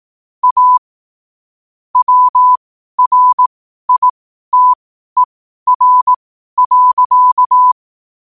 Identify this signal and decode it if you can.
Morse code, heard as a keyed tone, about 18 wpm, spaced as for 12 wpm, 1000 Hz. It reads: A WRITER.
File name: qotdfiles / 12wpm